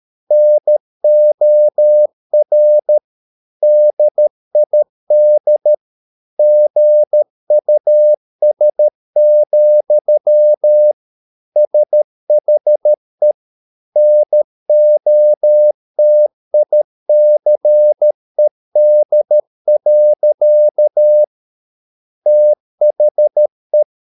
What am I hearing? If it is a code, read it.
Morse code, heard as a keyed tone, about 13 wpm, 600 Hz. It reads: NOR DID GUS, SHE NOTICED. THE